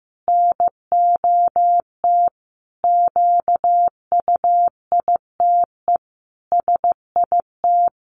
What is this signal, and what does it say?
Morse code, heard as a keyed tone, about 15 wpm, 700 Hz. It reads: NOT QUITE SIT
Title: A Sound Of Thunder 15wpm